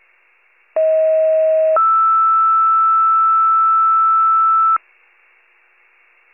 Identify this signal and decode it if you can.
Two-tone page, heard as a one-second tone, then a three-second tone, about 640 then 1320 Hz. It is heard from 0.8 to 4.8 s.
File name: pager / 38 FIRE / wcalert